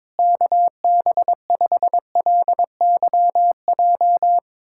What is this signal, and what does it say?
Morse code, heard as a keyed tone, about 21 wpm, 700 Hz. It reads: KB5LYJ
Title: USA Callsign DE Callsign 22wpm